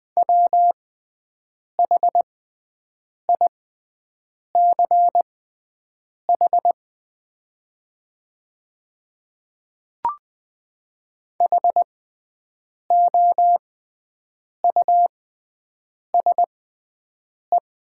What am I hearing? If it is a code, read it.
Morse code, heard as a keyed tone, about 20 wpm, spaced as for 7 wpm, 700 Hz. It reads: WHICH HOUSE
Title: 6x Character Spacing - Top 200 Words - 20wpm